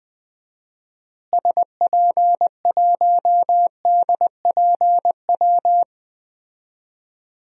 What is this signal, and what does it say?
Morse code, heard as a keyed tone, about 21 wpm, 700 Hz. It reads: SP1DPW
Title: DXCC-100 Call Signs - 6 Characters - Repeated 2x - 20wpm